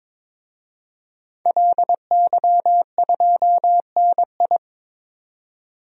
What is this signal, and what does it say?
Morse code, heard as a keyed tone, about 22 wpm, 700 Hz. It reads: LY2NI
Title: DXCC-100 Call Signs - 5 Characters - 22wpm